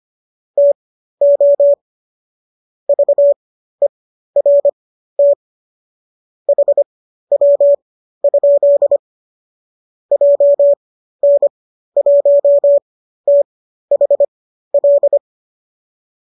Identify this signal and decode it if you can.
Morse code, heard as a keyed tone, about 25 wpm, spaced as for 13 wpm, 570 Hz. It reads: TO VERT HW? JN1THL